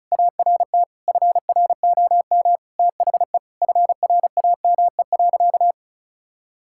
Morse code, heard as a keyed tone, about 36 wpm, 700 Hz. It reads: ART FROM THE FRAME.
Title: Sentences from Top 1000 Words - Review Part 6 - 35wpm